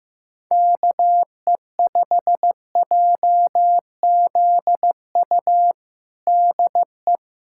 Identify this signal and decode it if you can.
Morse code, heard as a keyed tone, about 15 wpm, 700 Hz. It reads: KE5JZU DE
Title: USA Callsign DE Callsign 15wpm